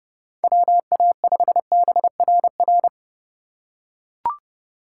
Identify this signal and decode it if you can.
Morse code, heard as a keyed tone, about 30 wpm, 700 Hz. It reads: WA5BRR
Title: USA Call Signs - 6 Characters - 2x Character Spacing - 30wpm